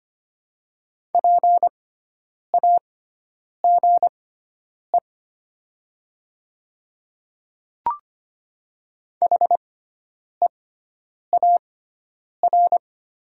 Morse code, heard as a keyed tone, about 25 wpm, spaced as for 9 wpm, 700 Hz. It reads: PAGE HEAR
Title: 6x Character Spacing - Top 300 Words - 25wpm